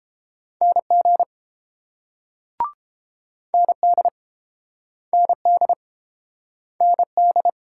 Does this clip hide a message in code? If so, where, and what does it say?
Yes: Morse code from 0.6 to 7.5 s, heard as a keyed tone, about 30 wpm, 690 Hz. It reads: NG ND ND ND